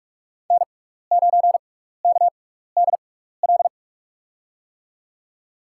Morse code, heard as a keyed tone, about 46 wpm, spaced as for 16 wpm, 700 Hz. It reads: N9KDL
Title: USA Call Signs - 6x Character Spacing - 45wpm